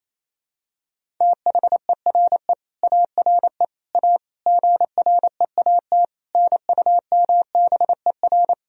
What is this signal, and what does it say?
Morse code, heard as a keyed tone, about 28 wpm, 700 Hz. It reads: THERE ARE A GREAT NUMBER
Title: Sentences from Top 1000 Words - Review Part 1 - 28wpm